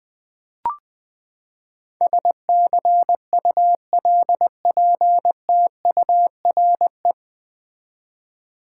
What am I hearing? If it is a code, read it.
Morse code, heard as a keyed tone, about 20 wpm, 700 Hz. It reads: SCULPTURE
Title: Nine Letter Words - 20wpm